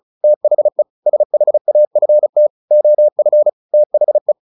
Morse code, heard as a keyed tone, about 34 wpm, 600 Hz. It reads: THE SHAFT OF THE
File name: the old man and the sea 2 35wpm